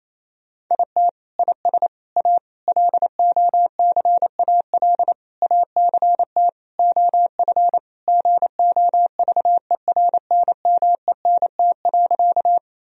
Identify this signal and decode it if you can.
Morse code, heard as a keyed tone, about 28 wpm, 700 Hz. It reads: IT IS A LOCAL ACT OF GOVERNMENT.
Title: Sentences from Top 1000 Words - Review Part 3 - 28wpm